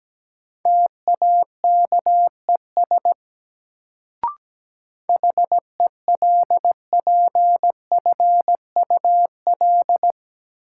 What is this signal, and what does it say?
Morse code, heard as a keyed tone, about 17 wpm, 700 Hz. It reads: TAKES HELPFUL
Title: Sentences from Top 1000 Words - Review Part 6 - New Word Drill - 17wpm